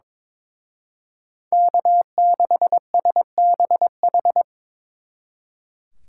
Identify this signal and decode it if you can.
Morse code, heard as a keyed tone, about 25 wpm, 700 Hz. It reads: K6SBH
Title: Speed Racing - USA Call Signs 22wpm